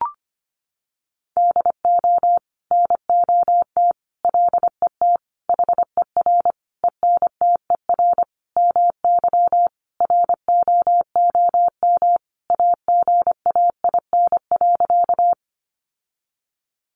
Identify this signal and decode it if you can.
Morse code, heard as a keyed tone, about 25 wpm, 700 Hz. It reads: DO NOT LET HER ENTER MY ROOM AGAIN.
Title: Sentences from Top 1300 Words 25wpm